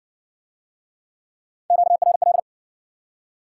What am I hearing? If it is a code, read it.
Morse code, heard as a keyed tone, about 60 wpm, 700 Hz. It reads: QRL